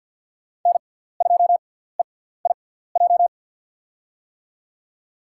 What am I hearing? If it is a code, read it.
Morse code, heard as a keyed tone, about 51 wpm, spaced as for 17 wpm, 690 Hz. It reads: N2EIJ
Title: USA Call Signs - 5 Characters - 6x Character Spacing - 50wpm